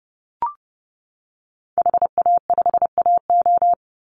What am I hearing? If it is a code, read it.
Morse code, heard as a keyed tone, about 30 wpm, 700 Hz. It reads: HA5AO
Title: Call Signs - 30wpm